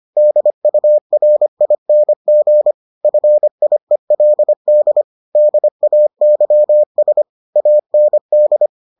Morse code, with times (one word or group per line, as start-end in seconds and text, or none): 0.2-2.7 DURING
3.0-5.0 FIELD
5.4-7.2 DAYS
7.6-8.7 AND